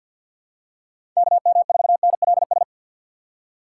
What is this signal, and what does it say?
Morse code, heard as a keyed tone, about 45 wpm, 700 Hz. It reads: KM4NLS
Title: USA Call Signs - 6x Character Spacing - 50wpm